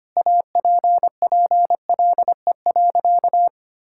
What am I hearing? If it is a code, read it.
Morse code, heard as a keyed tone, about 25 wpm, 700 Hz. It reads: APPLE.
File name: practice / 25wpm / Sentences with 2-Word Phrases - 25wpm